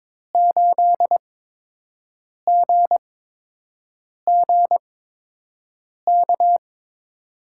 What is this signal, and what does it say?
Morse code, heard as a keyed tone, about 21 wpm, spaced as for 6 wpm, 700 Hz. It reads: 8GGK
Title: USA Call Signs - 6 Characters - 8x Character Spacing - 22wpm